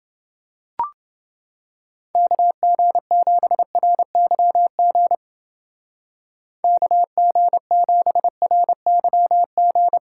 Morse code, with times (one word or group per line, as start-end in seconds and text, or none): 2.2-5.2 KG7RYG
6.6-10.0 KG7RYG